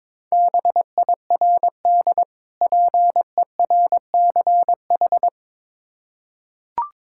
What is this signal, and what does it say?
Morse code, heard as a keyed tone, about 22 wpm, 700 Hz. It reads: BIRD PERCH
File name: Difficult Words with Hints - 22wpm